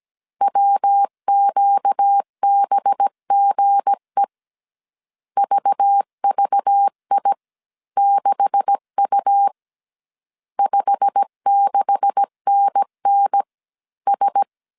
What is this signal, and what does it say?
Morse code, heard as a keyed tone, about 17 wpm, spaced as for 15 wpm, 800 Hz. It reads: WQBGE VVI6U 56NNS
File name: CW / DET / 1 9 1500C S C 16wpm 2909